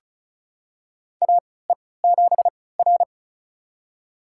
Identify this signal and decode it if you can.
Morse code, heard as a keyed tone, about 36 wpm, spaced as for 20 wpm, 700 Hz. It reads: AE7R
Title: USA Call Signs - 3x Character Spacing - 35wpm